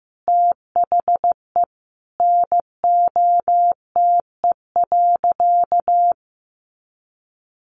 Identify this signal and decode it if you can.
Morse code, heard as a keyed tone, about 15 wpm, 700 Hz. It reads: THE NOTE.
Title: Sentences from Top 500 Words 15wpm